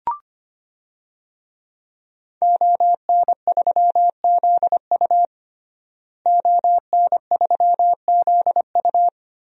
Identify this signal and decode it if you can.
Morse code, heard as a keyed tone, about 25 wpm, 700 Hz. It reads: ON3ZU ON3ZU
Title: DXCC-100 Call Signs - 5 Characters - Repeated 3x - 25wpm